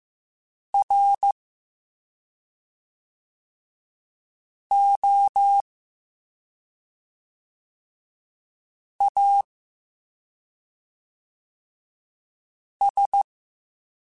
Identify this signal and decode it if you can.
Morse code, heard as a keyed tone, about 15 wpm, spaced as for 2 wpm, 800 Hz. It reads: ROAS